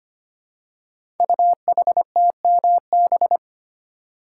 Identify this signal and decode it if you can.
Morse code, heard as a keyed tone, about 25 wpm, 700 Hz. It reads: UHTMB
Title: Code Groups - 5 Letter:Numbers - 25wpm